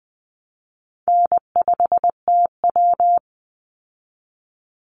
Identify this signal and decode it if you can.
Morse code, heard as a keyed tone, about 20 wpm, 700 Hz. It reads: N5TW
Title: Easy Call Signs Repeated 2x - 20wpm